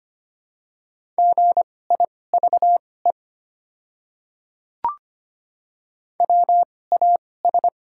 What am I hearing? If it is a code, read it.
Morse code, heard as a keyed tone, about 25 wpm, spaced as for 18 wpm, 700 Hz. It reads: GIVE WAS
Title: 2x Character Spacing - Top 200 Words - 25wpm